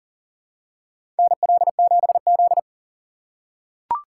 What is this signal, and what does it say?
Morse code, heard as a keyed tone, about 40 wpm, 700 Hz. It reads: NL7Z
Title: USA Call Signs - 4 Characters - 4x Character Spacing - 40wpm